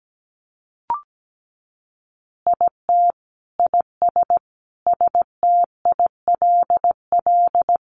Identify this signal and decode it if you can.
Morse code, heard as a keyed tone, about 17 wpm, 700 Hz. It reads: IT IS STILL
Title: Easy Phrases 17wpm